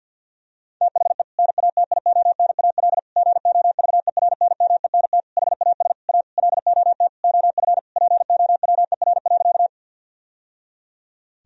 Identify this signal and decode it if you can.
Morse code, heard as a keyed tone, about 52 wpm, 700 Hz. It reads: THE NATIONAL GOVERNMENT HAS A LOT OF POWER.
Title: Sentences from Top 1000 Words - Review Part 4 - 50wpm